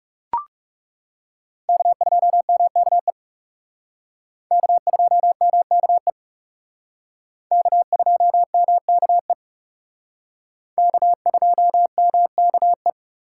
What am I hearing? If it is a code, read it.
Morse code, heard as a keyed tone, about 37 wpm, 700 Hz. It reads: K2MKE K2MKE K2MKE K2MKE
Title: Speed Racing - USA 1x3 Call Signs 30wpm